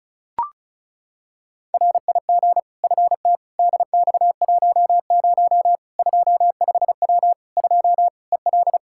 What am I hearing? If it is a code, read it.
Morse code, heard as a keyed tone, about 36 wpm, 700 Hz. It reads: RIG FT DX10 25W 2 EL
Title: Practice QSO - Exchange 2 - Info - 35wpm